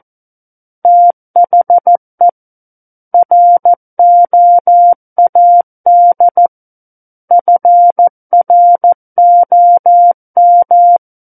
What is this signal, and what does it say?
Morse code, heard as a keyed tone, about 14 wpm, 700 Hz. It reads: THE ROAD FROM
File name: Book One / 05-Chapter 5 - 14 WPM